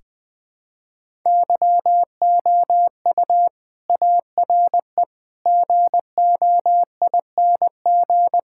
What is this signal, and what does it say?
Morse code, heard as a keyed tone, about 20 wpm, 690 Hz. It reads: YOU ARE GOING
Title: Build A Phrase - 4-Word Phrases - 20wpm